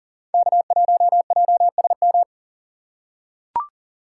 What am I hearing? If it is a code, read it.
Morse code, heard as a keyed tone, about 40 wpm, 700 Hz. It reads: K1JSM
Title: USA Call Signs - 5 Characters - 4x Character Spacing - 40wpm